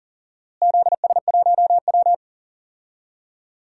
Morse code, heard as a keyed tone, about 40 wpm, 700 Hz. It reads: ZS1W
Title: DXCC-100 Call Signs - 4 Characters - Repeated 2x - 40wpm